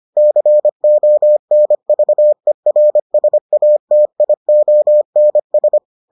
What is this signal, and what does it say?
Morse code, heard as a keyed tone, about 25 wpm, 600 Hz. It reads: CONVERSATIONS